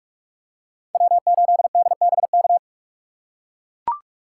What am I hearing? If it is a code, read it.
Morse code, heard as a keyed tone, about 43 wpm, 700 Hz. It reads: W8DBK
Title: USA Call Signs - 5 Characters - 3x Character Spacing - 45wpm